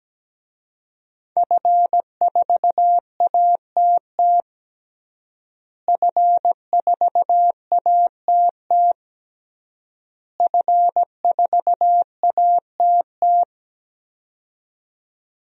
Morse code, heard as a keyed tone, about 17 wpm, 700 Hz. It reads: F4ATT F4ATT F4ATT
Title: DXCC-100 Call Signs - 5 Characters - Repeated 3x - 17wpm